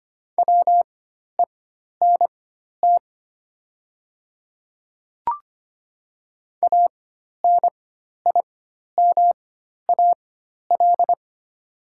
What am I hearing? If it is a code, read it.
Morse code, heard as a keyed tone, about 25 wpm, spaced as for 12 wpm, 700 Hz. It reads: WENT ANIMAL
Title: 4x Character Spacing - Top 300 Words - 25wpm